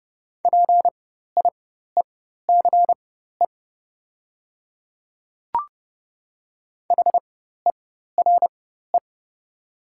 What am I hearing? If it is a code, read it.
Morse code, heard as a keyed tone, about 30 wpm, spaced as for 14 wpm, 700 Hz. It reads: PIECE HERE
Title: 4x Character Spacing - Top 500 Words - 30wpm